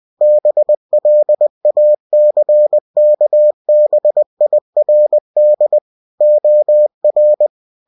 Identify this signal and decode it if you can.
Morse code, heard as a keyed tone, about 20 wpm, 600 Hz. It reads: BLACKBIRD OR
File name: cw-books / winnie-the-pooh / winnie-20wpm-page-0069